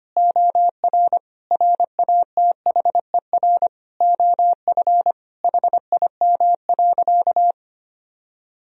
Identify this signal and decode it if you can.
Morse code, heard as a keyed tone, about 25 wpm, 700 Hz. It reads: OR RATHER OF HIM.
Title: Sentences from Top 1000 Words - Review Part 4 - 25wpm